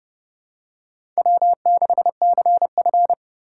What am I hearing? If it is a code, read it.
Morse code, heard as a keyed tone, about 30 wpm, 700 Hz. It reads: W6CF
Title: USA Call Signs - 4 Characters - 3x Character Spacing - 30wpm